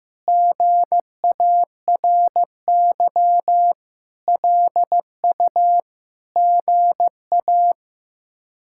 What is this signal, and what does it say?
Morse code, heard as a keyed tone, about 15 wpm, 700 Hz. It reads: GARY LU GA
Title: K1USN SST QSO Practice - 15wpm